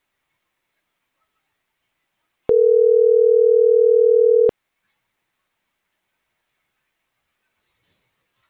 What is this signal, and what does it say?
Ringback tone (North America), 440 + 480 Hz.